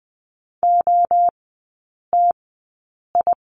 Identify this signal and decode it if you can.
Morse code, heard as a keyed tone, about 20 wpm, spaced as for 8 wpm, 700 Hz. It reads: OTI
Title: 047 Lesson 9e - Number 4 - Three Characters 20wpm